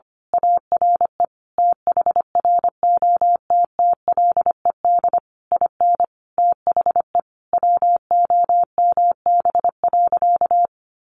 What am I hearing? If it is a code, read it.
Morse code, heard as a keyed tone, about 25 wpm, 700 Hz. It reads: ARE THROTTLED IN THE WOMB.
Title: A Sound Of Thunder 25wpm